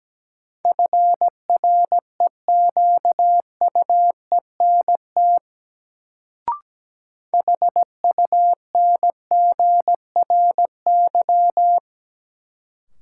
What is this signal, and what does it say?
Morse code, heard as a keyed tone, about 17 wpm, 700 Hz. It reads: FREQUENT HUNGRY
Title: Sentences from Top 1400 Words - New Word Drill - 17wpm